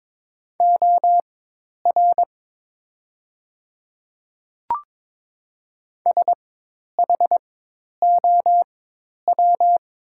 Morse code, heard as a keyed tone, about 22 wpm, spaced as for 10 wpm, 690 Hz. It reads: OR SHOW
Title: 4x Character Spacing - Top 300 Words - 22wpm